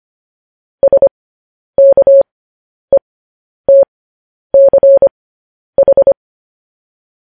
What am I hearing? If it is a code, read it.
Morse code, heard as a keyed tone, about 25 wpm, spaced as for 10 wpm, 550 Hz. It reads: SKETCH
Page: news headlines in morse code at 10 wpm